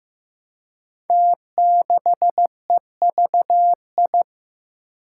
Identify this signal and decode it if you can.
Morse code, heard as a keyed tone, about 15 wpm, 700 Hz. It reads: T6EVI
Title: Five Challenging Letter-Number Combinations 15wpm